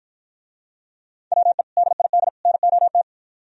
Morse code, heard as a keyed tone, about 54 wpm, 700 Hz. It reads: WE DID NOT